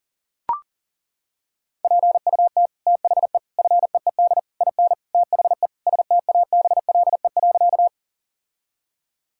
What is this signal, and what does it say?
Morse code, heard as a keyed tone, about 40 wpm, 700 Hz. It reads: PUT THE FEED IN THE STABLE.
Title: Sentences from Top 1600 Words 40wpm